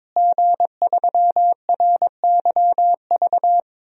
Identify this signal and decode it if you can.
Morse code, heard as a keyed tone, about 22 wpm, 700 Hz. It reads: G3RYV
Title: DXCC-100 Call Signs - 5 Characters - Repeated 2x - 22wpm